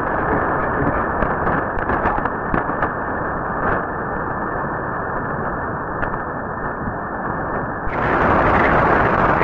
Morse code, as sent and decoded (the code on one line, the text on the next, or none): none